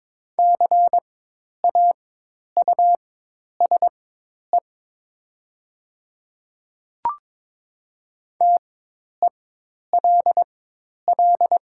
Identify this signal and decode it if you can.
Morse code, heard as a keyed tone, about 21 wpm, spaced as for 10 wpm, 700 Hz. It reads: CAUSE TELL
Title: 4x Character Spacing - Top 200 Words - 22wpm